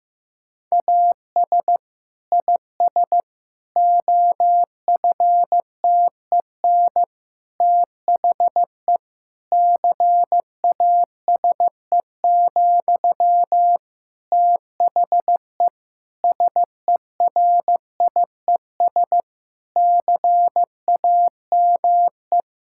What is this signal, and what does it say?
Morse code, heard as a keyed tone, about 15 wpm, 700 Hz. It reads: AS IS OFTEN THE CASE, THE SERIES CAME
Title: Sentences from Top 1000 Words - Review Part 5 - 15wpm